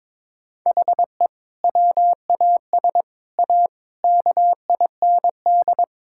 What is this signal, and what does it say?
Morse code, heard as a keyed tone, about 22 wpm, 700 Hz. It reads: HE WAS A KIND
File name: practice / 22wpm / Sentences with 3-Word Phrases - 22wpm